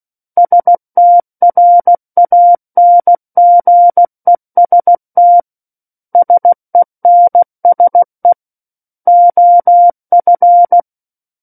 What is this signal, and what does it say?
Morse code, heard as a keyed tone, about 16 wpm, 700 Hz. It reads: STRANGEST SENSE OF
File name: Book One / 07-Chapter 7 - 16 WPM